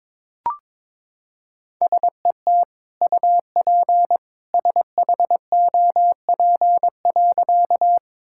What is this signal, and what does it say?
Morse code, heard as a keyed tone, about 22 wpm, 700 Hz. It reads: SET UP SHOP.
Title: Sentences from Top 1600 Words 22wpm